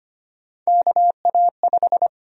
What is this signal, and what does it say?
Morse code, heard as a keyed tone, about 25 wpm, 700 Hz. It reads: KA5
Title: USA Call Sign Prefixes - 25wpm